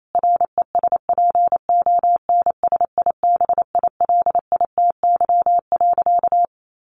Morse code, heard as a keyed tone, about 28 wpm, 700 Hz. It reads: RESPONSIBILITY.
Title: Sentences from Top 1500 Words 28wpm